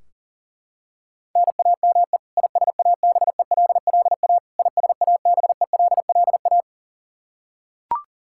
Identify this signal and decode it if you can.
Morse code, heard as a keyed tone, about 40 wpm, 700 Hz. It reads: NAME ISABELLA ISABELLA
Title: Practice QSO - Exchange 1 - Beginning - 40wpm